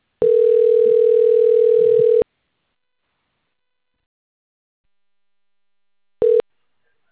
Ringback tone (North America), 440 + 480 Hz.
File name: rg-1600-unknown- - - .16776